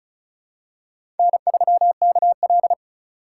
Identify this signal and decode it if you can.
Morse code, heard as a keyed tone, about 36 wpm, 700 Hz. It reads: N3KL
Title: USA Call Signs - 4 Characters - 2x Character Spacing - 35wpm